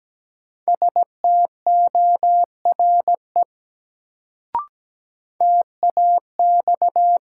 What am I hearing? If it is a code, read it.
Morse code, heard as a keyed tone, about 17 wpm, 700 Hz. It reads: STORE TAX
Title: Sentences from Top 1000 Words - Review Part 5 - New Word Drill - 17wpm